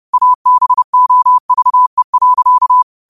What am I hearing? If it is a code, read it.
Morse code, heard as a keyed tone, about 30 wpm, 1000 Hz. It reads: ABOVE.
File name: qotdfiles / 30wpm